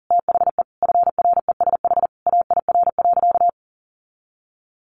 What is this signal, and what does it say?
Morse code, heard as a keyed tone, about 37 wpm, 700 Hz. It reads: THE FRESH AIR.
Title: Sentences from Top 1200 Words 40wpm